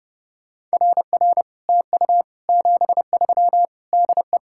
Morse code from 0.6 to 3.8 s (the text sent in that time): RR TU 73